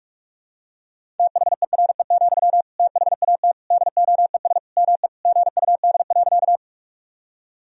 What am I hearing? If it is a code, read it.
Morse code, heard as a keyed tone, about 43 wpm, 700 Hz. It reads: THERE, THAT DOES ME GUD.
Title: Sentences from Top 200 Words and Abbreviations 45wpm